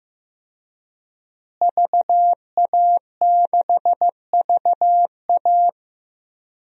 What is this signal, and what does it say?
Morse code, heard as a keyed tone, about 16 wpm, 700 Hz. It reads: VA6VA
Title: DXCC-100 Call Signs - 5 Characters - Repeated 3x - 15wpm